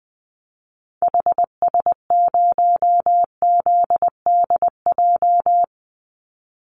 Morse code, heard as a keyed tone, about 20 wpm, 700 Hz. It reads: HS0ZDJ
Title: Call Signs - 20wpm